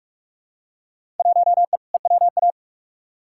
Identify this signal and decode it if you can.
Morse code, heard as a keyed tone, about 46 wpm, 700 Hz. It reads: 1E EWA
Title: Field Day Exchanges - 45wpm